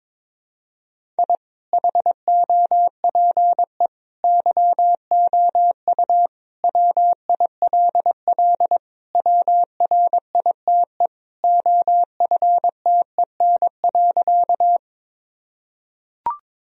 Morse code, heard as a keyed tone, about 22 wpm, 700 Hz. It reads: I HOPE YOU WILL WRITE OFTEN.
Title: Sentences from Top 1000 Words - Review Part 1 - 22wpm